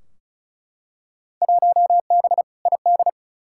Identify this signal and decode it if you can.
Morse code, heard as a keyed tone, about 36 wpm, 700 Hz. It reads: 1B ID
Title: Field Day Exchanges - 35wpm